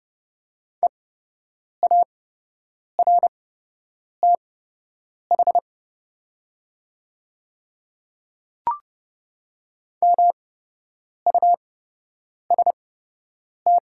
Morse code, heard as a keyed tone, about 30 wpm, spaced as for 8 wpm, 700 Hz. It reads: EARTH MUST
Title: 8x Character Spacing - Top 500 Words - 30wpm